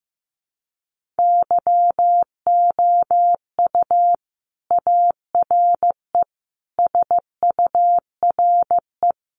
Morse code, heard as a keyed tone, about 15 wpm, 700 Hz. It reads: YOU ARE SURE
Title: Sentences from Top 300 Words 15wpm